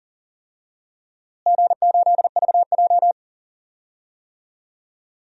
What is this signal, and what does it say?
Morse code, heard as a keyed tone, about 40 wpm, 700 Hz. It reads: G8VJ
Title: DXCC-100 Call Signs - 4 Characters - Repeated 3x - 40wpm